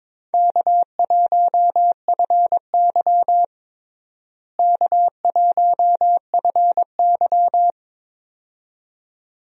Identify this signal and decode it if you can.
Morse code, heard as a keyed tone, about 22 wpm, 700 Hz. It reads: K1FY K1FY
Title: USA Call Signs - 4 Characters - Repeated 2x - 22wpm